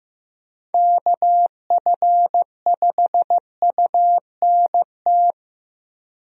Morse code, heard as a keyed tone, about 15 wpm, 700 Hz. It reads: KF5UNT
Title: USA Call Signs - 6 Characters - Repeated 3x - 15wpm